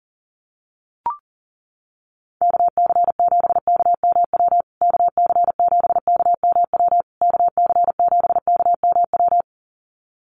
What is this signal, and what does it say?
Morse code, heard as a keyed tone, about 40 wpm, 700 Hz. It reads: KC7KMW KC7KMW KC7KMW
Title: USA Call Signs Repeated 3x 40wpm